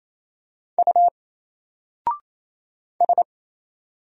Morse code, heard as a keyed tone, about 28 wpm, 700 Hz. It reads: U S
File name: Single-Letter-Number-28wpm